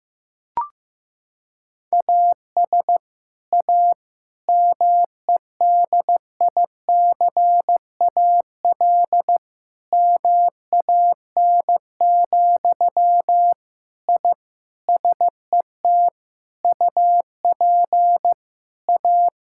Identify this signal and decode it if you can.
Morse code, heard as a keyed tone, about 15 wpm, 700 Hz. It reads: AS A MEDICAL MAN, I SET UP A
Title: Sentences from Top 1000 Words - Review Part 5 - 15wpm